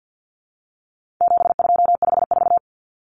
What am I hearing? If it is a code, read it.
Morse code, heard as a keyed tone, about 50 wpm, 700 Hz. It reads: 7254
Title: Numbers - 4-Digit 50wpm